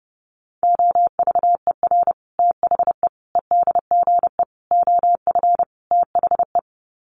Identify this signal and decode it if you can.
Morse code, heard as a keyed tone, about 30 wpm, 700 Hz. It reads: OVER THE EDGE OF THE
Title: Sets of 5 Words Encore2 30wpm